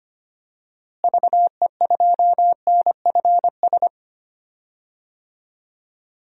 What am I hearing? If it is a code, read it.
Morse code, heard as a keyed tone, about 25 wpm, 700 Hz. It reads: VE2NFS
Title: DXCC-100 Call Signs - 6 Characters - Repeated 2x - 25wpm